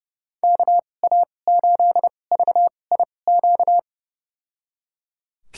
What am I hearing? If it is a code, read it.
Morse code, heard as a keyed tone, about 30 wpm, spaced as for 22 wpm, 700 Hz. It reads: KA8VIQ
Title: USA Call Signs - 6 Characters - 2x Character Spacing - 30wpm